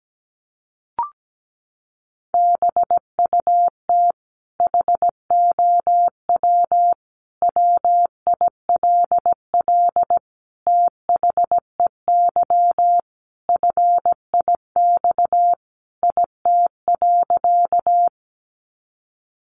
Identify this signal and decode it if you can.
Morse code, heard as a keyed tone, about 17 wpm, 700 Hz. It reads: BUT HOW WILL THEY FIX IT.
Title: Sentences from Top 1400 Words 17wpm